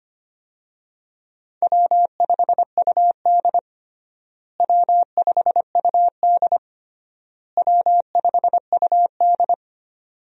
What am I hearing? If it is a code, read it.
Morse code, heard as a keyed tone, about 25 wpm, 700 Hz. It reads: W5UD W5UD W5UD
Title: USA Call Signs - 4 Characters - Repeated 3x - 25wpm